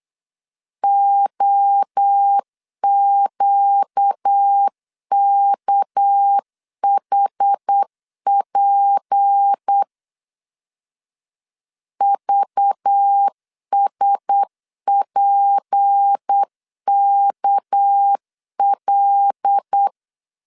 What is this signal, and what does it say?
Morse code, heard as a keyed tone, about 8 wpm, 800 Hz. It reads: OQKHP VSPKL
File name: LEZIONI / 6 PKX0 / 6 PK 8wpm 3003